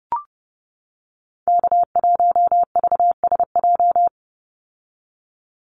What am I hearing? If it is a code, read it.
Morse code, heard as a keyed tone, about 30 wpm, 700 Hz. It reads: K1VSJ
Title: Call Signs - 30wpm